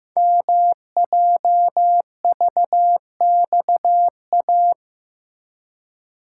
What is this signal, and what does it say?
Morse code, heard as a keyed tone, about 15 wpm, 700 Hz. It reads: MJVXA
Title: Code Groups - 5 Letters - 15wpm